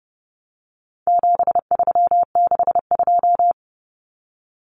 Morse code, heard as a keyed tone, about 30 wpm, 700 Hz. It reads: 7362
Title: Numbers - 4-Digit 30wpm